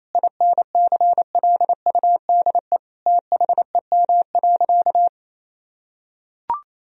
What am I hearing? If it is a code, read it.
Morse code, heard as a keyed tone, about 28 wpm, 700 Hz. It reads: INCLUDE THEM.
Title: Sentences from Top 1000 Words - Review Part 3 - 28wpm